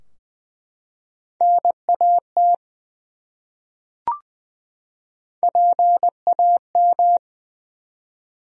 Morse code, heard as a keyed tone, about 20 wpm, 700 Hz. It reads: NAT PAM
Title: USA Common Names - Max 3 Letters - 20wpm